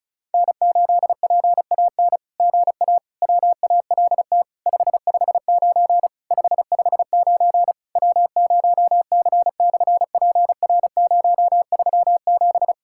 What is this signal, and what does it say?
Morse code, heard as a keyed tone, about 33 wpm, 700 Hz. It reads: N8PAN GA WALT 559 559 W0C/PR037
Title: SOTA QSO Practice 35wpm